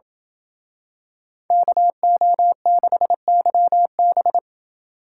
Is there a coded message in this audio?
Yes: Morse code from 1.5 to 4.4 s, heard as a keyed tone, about 28 wpm, 700 Hz. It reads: KO6YB